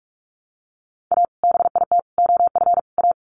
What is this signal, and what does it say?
Morse code, heard as a keyed tone, about 45 wpm, 700 Hz. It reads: A BIT OF A
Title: Sets of 4 Words 45wpm